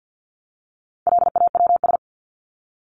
Morse code, heard as a keyed tone, about 50 wpm, 700 Hz. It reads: LAWS
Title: ES:S Suffix - Top 100 Words with Base Words 50wpm